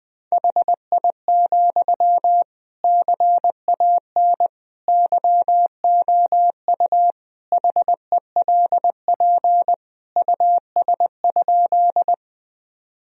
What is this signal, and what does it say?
Morse code, heard as a keyed tone, about 20 wpm, 700 Hz. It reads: HI, CAN YOU HELP US?
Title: Sentences from Top 1000 Words - Review Part 3 - 20wpm